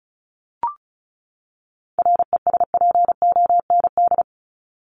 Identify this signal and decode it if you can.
Morse code, heard as a keyed tone, about 34 wpm, 700 Hz. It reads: RESPOND
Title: Top 1751-2000 Words 35wpm